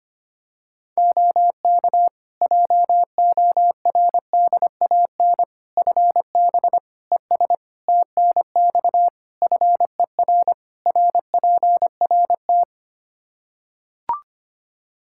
Morse code, heard as a keyed tone, about 25 wpm, 700 Hz. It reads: OK JORDAN FB ES TNX FER RPRT
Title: Practice QSO - Exchange 2 - Info - 25wpm